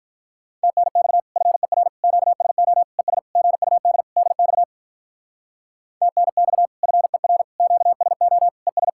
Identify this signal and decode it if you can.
Morse code, heard as a keyed tone, about 44 wpm, 700 Hz. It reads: TNX FER QSO ES GUD DX TNX FER QSO ES